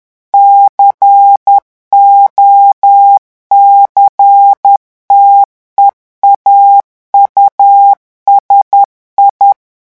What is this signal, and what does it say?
Morse code, heard as a keyed tone, about 10 wpm, 800 Hz. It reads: COCTEAU SI
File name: revision9 citations humoristiques vitesse 10 mots minute